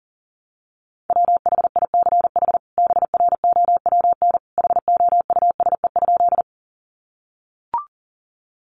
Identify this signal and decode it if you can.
Morse code, heard as a keyed tone, about 40 wpm, 700 Hz. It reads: WHICH BROWN HOUSE?
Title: Sentences from Top 1200 Words 40wpm